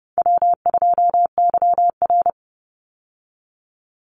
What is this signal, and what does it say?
Morse code, heard as a keyed tone, about 30 wpm, 700 Hz. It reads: W2YR
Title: Easy Call Signs Repeated 2x - 30wpm